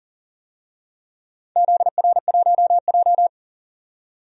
Morse code, heard as a keyed tone, about 37 wpm, 700 Hz. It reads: ZR1J
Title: DXCC-100 Call Signs - 4 Characters - 40wpm